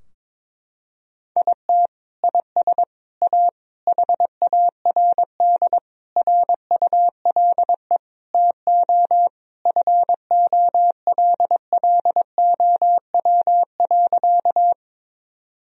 Morse code, heard as a keyed tone, about 22 wpm, 700 Hz. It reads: IT IS A HARD RULE TO FOLLOW.
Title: Sentences from Top 1000 Words - Review Part 2 - 22wpm